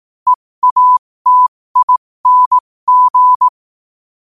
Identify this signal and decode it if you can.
Morse code, heard as a keyed tone, about 18 wpm, spaced as for 15 wpm, 1000 Hz. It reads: EATING
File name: qotdfiles / 15wpm